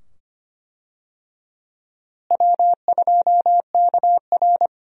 Morse code, heard as a keyed tone, about 25 wpm, 700 Hz. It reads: W2KR
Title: USA Call Signs - 4 Characters - Repeated 2x - 25wpm